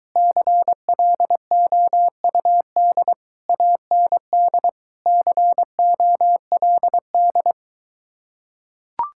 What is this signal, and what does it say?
Morse code, heard as a keyed tone, about 23 wpm, 700 Hz. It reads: CLOUD AND COLD